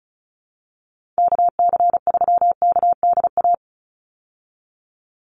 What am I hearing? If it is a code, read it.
Morse code, heard as a keyed tone, about 34 wpm, 700 Hz. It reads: KC3KDA
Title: USA Call Signs 35wpm